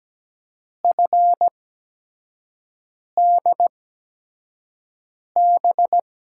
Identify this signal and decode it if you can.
Morse code, heard as a keyed tone, about 17 wpm, spaced as for 5 wpm, 700 Hz. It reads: FDB
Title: USA Call Signs - 8x Character Spacing - 17wpm